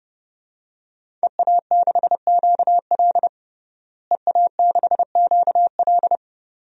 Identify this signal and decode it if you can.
Morse code, heard as a keyed tone, about 32 wpm, 700 Hz. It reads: EA6QL EA6QL
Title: DXCC-100 Call Signs - 5 Characters - Repeated 2x - 30wpm